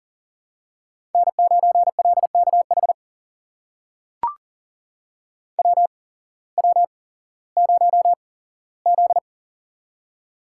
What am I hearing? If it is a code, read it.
Morse code, heard as a keyed tone, about 40 wpm, spaced as for 19 wpm, 700 Hz. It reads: N9LKH WW0Z
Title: USA Call Signs - 8x Character Spacing - 40wpm